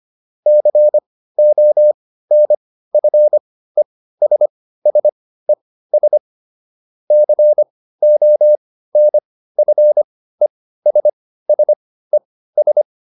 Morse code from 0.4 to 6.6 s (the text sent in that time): CONFESSES